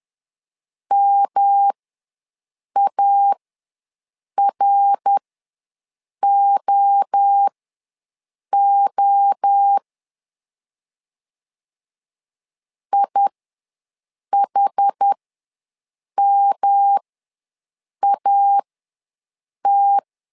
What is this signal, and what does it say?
Morse code, heard as a keyed tone, about 10 wpm, spaced as for 6 wpm, 800 Hz. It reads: MAROO IHMAT
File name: LEZIONI / 3 MWRO / 3 MWRO ws 10wpm 2310